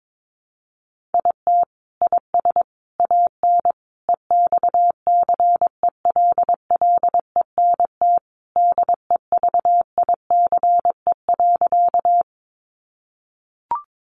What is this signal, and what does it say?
Morse code, heard as a keyed tone, about 22 wpm, 700 Hz. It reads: IT IS AN EXCELLENT DEVICE.
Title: Sentences from Top 1500 Words 22wpm